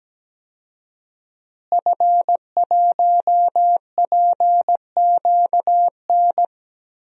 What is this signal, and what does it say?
Morse code, heard as a keyed tone, about 17 wpm, 700 Hz. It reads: F1PQN
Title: DXCC-100 Call Signs - 5 Characters - Repeated 3x - 17wpm